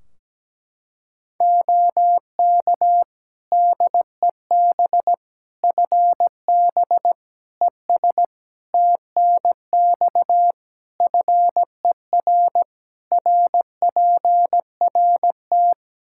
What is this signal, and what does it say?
Morse code, heard as a keyed tone, about 17 wpm, 700 Hz. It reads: OK DEB FB ES TNX FER RPRT
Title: Practice QSO - Exchange 2 - Info - 17wpm